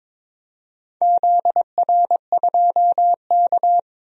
Morse code, heard as a keyed tone, about 22 wpm, 700 Hz. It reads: ZR2K
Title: DXCC-100 Call Signs - 4 Characters - 22wpm